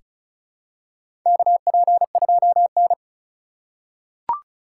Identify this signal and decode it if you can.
Morse code, heard as a keyed tone, about 36 wpm, 700 Hz. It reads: KP2N
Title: K1USN SST QSO Practice - 35wpm